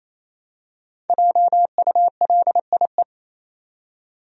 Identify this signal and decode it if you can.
Morse code, heard as a keyed tone, about 28 wpm, 700 Hz. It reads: JULIE
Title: USA Common Names - 28wpm